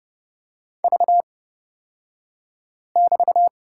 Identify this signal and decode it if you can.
Morse code, heard as a keyed tone, about 29 wpm, spaced as for 5 wpm, 700 Hz. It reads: V<BT>
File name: Lesson 36d - BT - Two Characters 30wpm